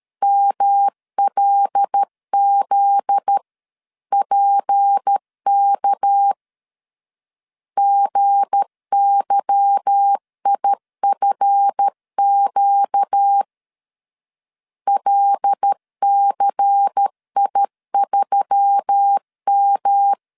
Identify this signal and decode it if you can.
Morse code, heard as a keyed tone, about 13 wpm, 800 Hz. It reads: MLZPK GYIFQ LCI3M
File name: LEZIONI / 8 YJ45 / 8 Y 12wpm 2228